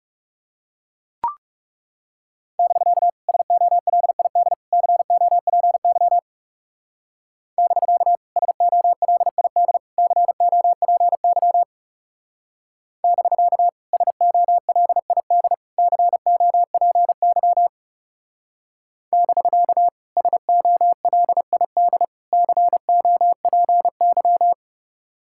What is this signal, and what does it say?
Morse code, heard as a keyed tone, about 36 wpm, 700 Hz. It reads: <BK> SOLID COPY <BK> SOLID COPY <BK> SOLID COPY <BK> SOLID COPY
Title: Speed-Racing Common QSO Phrases - 30wpm